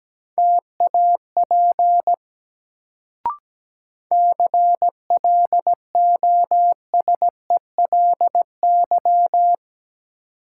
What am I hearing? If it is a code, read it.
Morse code, heard as a keyed tone, about 17 wpm, 700 Hz. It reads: TAP CLOSELY
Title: Sentences from Top 1400 Words - New Word Drill - 17wpm